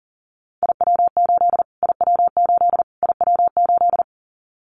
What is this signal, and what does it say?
Morse code, heard as a keyed tone, about 40 wpm, 700 Hz. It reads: IW8 IW8 IW8
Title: Callsign Prefixes for Top 25 DXCC - Repeated 3x - 40wpm